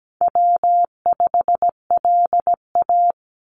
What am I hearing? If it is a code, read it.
Morse code, heard as a keyed tone, about 17 wpm, 700 Hz. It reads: W5LA
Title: Call Signs - 17wpm